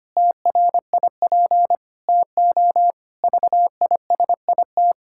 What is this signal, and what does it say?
Morse code, heard as a keyed tone, about 25 wpm, 700 Hz. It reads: TRIP TO VISIT
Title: Pangram Challenge v2 - 25wpm